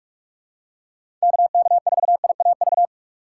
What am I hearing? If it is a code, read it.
Morse code, heard as a keyed tone, about 46 wpm, 700 Hz. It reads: KK4IAV
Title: USA Call Signs - 6 Characters - 3x Character Spacing - 45wpm